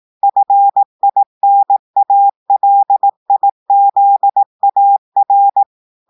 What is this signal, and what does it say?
Morse code, heard as a keyed tone, about 18 wpm, 800 Hz. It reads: FINALIZAR
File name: audios / portugais / portugais texte en clair 2 vitesse 18 mots minutes